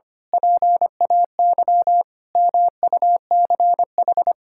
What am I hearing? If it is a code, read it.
Morse code, heard as a keyed tone, about 25 wpm, 700 Hz. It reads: PAY MUCH
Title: Sentences from Top 1000 Words - Review Part 6 - 25wpm